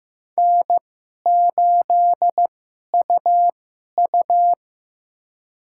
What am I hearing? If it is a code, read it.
Morse code, heard as a keyed tone, about 15 wpm, spaced as for 11 wpm, 700 Hz. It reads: N8UU
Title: USA Call Signs - 4 Characters - 2x Character Spacing - 15wpm